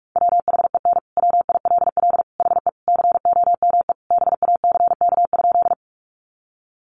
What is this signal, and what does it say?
Morse code, heard as a keyed tone, about 44 wpm, 700 Hz. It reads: WHEN WILL HE COME BACK?
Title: Sentences from Top 200 Words 45wpm